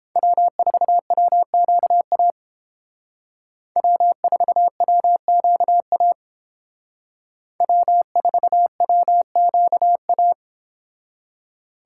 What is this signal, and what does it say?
Morse code, heard as a keyed tone, about 31 wpm, 700 Hz. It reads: W4WQA W4WQA W4WQA